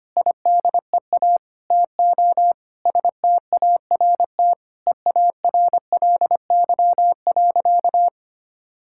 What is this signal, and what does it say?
Morse code, heard as a keyed tone, about 25 wpm, 700 Hz. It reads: IDEA TO START EARLY.
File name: practice / 25wpm / Sentences with 9-Word Phrases - 25wpm